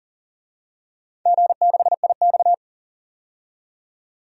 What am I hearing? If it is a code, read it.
Morse code, heard as a keyed tone, about 37 wpm, 700 Hz. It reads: G6IX
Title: DXCC-100 Call Signs - 4 Characters - 40wpm